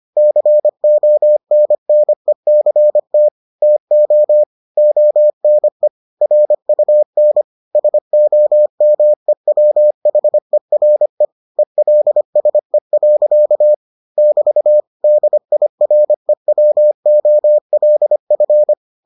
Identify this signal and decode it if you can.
Morse code, heard as a keyed tone, about 25 wpm, 600 Hz. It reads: CONNECT TO ONE RUN SOMEWHERE ELSE. <BT> DIREWOLF